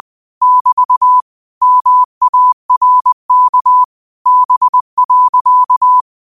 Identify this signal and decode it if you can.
Morse code, heard as a keyed tone, about 20 wpm, 1000 Hz. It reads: <BT> MARK B.
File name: qotdfiles / 20wpm